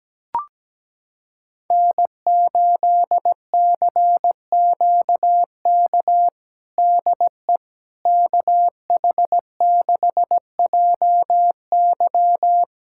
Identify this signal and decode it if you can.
Morse code, heard as a keyed tone, about 17 wpm, 700 Hz. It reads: N8CQK DE KH6JY
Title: USA Callsign DE Callsign - 5 Characters - 17wpm